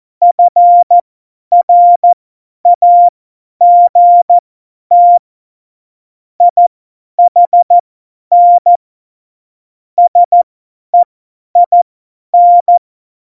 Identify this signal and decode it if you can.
Morse code, heard as a keyed tone, about 14 wpm, spaced as for 10 wpm, 700 Hz. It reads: FRAGT IHN SEIN
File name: Morsekurs / Witze / Witz 26. Januar 2021, 50 F70